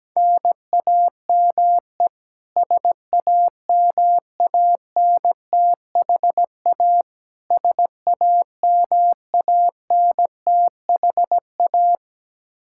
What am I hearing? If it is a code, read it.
Morse code, heard as a keyed tone, about 17 wpm, 700 Hz. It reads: NAME SAMANTHA SAMANTHA
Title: Practice QSO - Exchange 1 - Beginning - 17wpm